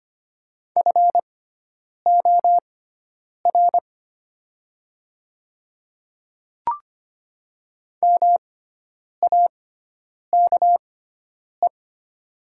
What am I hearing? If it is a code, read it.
Morse code, heard as a keyed tone, about 25 wpm, spaced as for 9 wpm, 700 Hz. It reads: FOR MAKE
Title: 6x Character Spacing - Top 300 Words - 25wpm